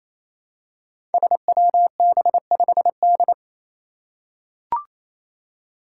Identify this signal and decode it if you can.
Morse code, heard as a keyed tone, about 27 wpm, 700 Hz. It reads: SWB5D
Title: Five Challenging Letter-Number Combinations 28wpm